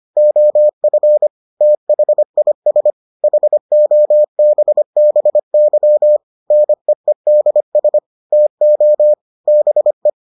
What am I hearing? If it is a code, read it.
Morse code, heard as a keyed tone, about 25 wpm, 600 Hz. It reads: OF THIS HOBBY NEEDS TO BE